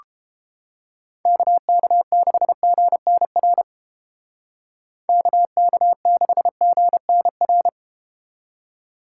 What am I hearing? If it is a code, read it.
Morse code, heard as a keyed tone, about 33 wpm, 700 Hz. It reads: KK6GNR KK6GNR